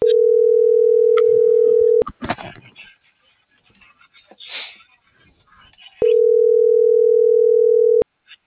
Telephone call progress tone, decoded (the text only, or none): ringback tone